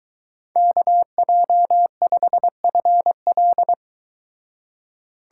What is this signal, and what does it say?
Morse code, heard as a keyed tone, about 23 wpm, 700 Hz. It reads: KJ5FL